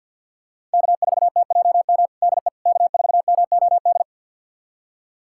Morse code, heard as a keyed tone, about 48 wpm, 700 Hz. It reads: K4TJM DE K4MOD
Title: USA Callsign DE Callsign - 5 Characters - 50wpm